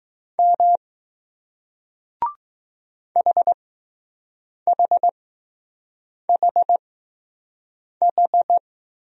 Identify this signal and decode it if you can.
Morse code, heard as a keyed tone, about 20 wpm, spaced as for 6 wpm, 700 Hz. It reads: M HHHH